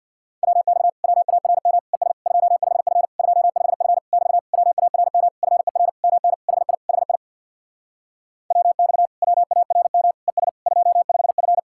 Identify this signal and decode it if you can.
Morse code, heard as a keyed tone, about 66 wpm, 700 Hz. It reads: WX WARM ES 25F 25F <BT> WARM FER MN HI HI WX WARM ES 25F